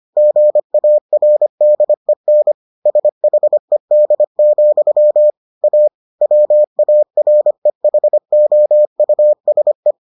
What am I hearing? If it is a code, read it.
Morse code, heard as a keyed tone, about 25 wpm, 600 Hz. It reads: GARDEN SHED, A WAREHOUSE